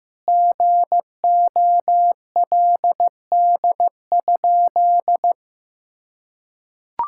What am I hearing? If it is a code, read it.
Morse code, heard as a keyed tone, about 15 wpm, 700 Hz. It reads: GOLD?
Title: Sentences from Top 1600 Words 15wpm